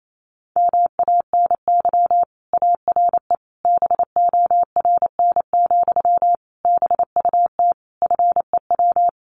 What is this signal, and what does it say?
Morse code, heard as a keyed tone, about 28 wpm, 700 Hz. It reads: MANY ARE BORN, BUT FEW
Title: Sentences from Top 1300 Words 28wpm